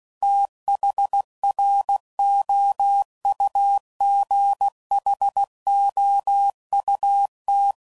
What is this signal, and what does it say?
Morse code, heard as a keyed tone, about 16 wpm, 800 Hz. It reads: THROUGHOUT
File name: CW Code Groups @15WPM / Weather Sets / Weather 5 12